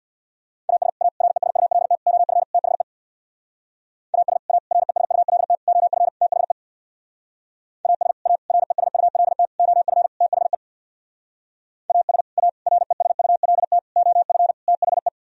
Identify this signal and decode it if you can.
Morse code, heard as a keyed tone, about 64 wpm, 700 Hz. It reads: AS A RESULT OF THE AS A RESULT OF THE AS A RESULT OF THE AS A RESULT OF THE